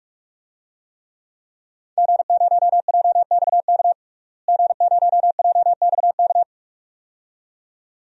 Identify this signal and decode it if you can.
Morse code, heard as a keyed tone, about 46 wpm, 700 Hz. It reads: G0JXK G0JXK
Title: DXCC-100 Call Signs - 5 Characters - Repeated 2x - 45wpm